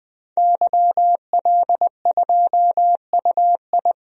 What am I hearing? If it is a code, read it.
Morse code, heard as a keyed tone, about 20 wpm, 700 Hz. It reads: YL2UI
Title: DXCC-100 Call Signs - 5 Characters - 20wpm